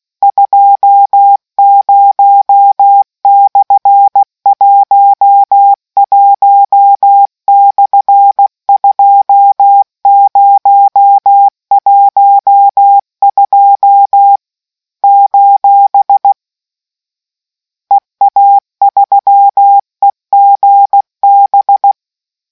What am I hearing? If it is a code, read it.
Morse code, heard as a keyed tone, about 16 wpm, 800 Hz. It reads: 20/11/2012 : EA3EGB